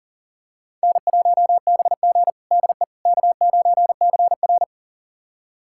Morse code, heard as a keyed tone, about 37 wpm, 700 Hz. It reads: N1BG DE K9CR
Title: USA Callsign DE Callsign - 4 Characters - 40wpm